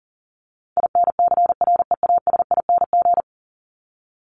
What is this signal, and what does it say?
Morse code, heard as a keyed tone, about 40 wpm, 700 Hz. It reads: INCREASING
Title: ING Suffix - Top 100 Words with Base Words 40wpm